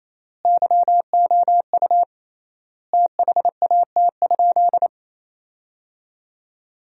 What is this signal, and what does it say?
Morse code, heard as a keyed tone, about 28 wpm, 700 Hz. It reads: YOU THAT?
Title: 3x Word-Spacing - Sentences from Top 400 Words - 28wpm